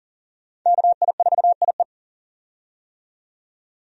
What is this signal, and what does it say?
Morse code, heard as a keyed tone, about 40 wpm, 700 Hz. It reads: KI4IE
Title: USA Call Signs - 5 Characters - 40wpm